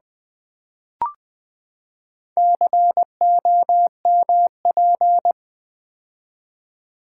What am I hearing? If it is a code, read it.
Morse code, heard as a keyed tone, about 20 wpm, 700 Hz. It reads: COMP
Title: Build A Word - 4 Letters At A Time - 10-Letter Words - 20wpm